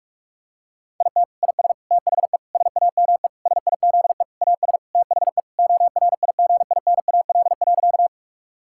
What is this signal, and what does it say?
Morse code, heard as a keyed tone, about 46 wpm, 700 Hz. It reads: IT IS THE SAME SIZE AS THE ORIGINAL.
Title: Sentences from Top 1000 Words - Review Part 6 - 45wpm